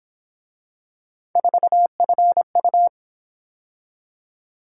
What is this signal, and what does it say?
Morse code, heard as a keyed tone, about 27 wpm, 700 Hz. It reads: 4FU
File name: Speed-Racing - Three Letter-Number Combinations - 22wpm